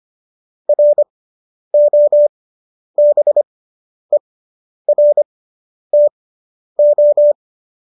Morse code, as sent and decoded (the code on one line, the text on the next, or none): .-. --- -... .-. - ---
ROBERTO